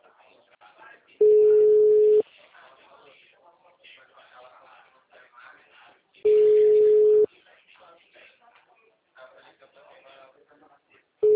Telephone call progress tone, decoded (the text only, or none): ringback tone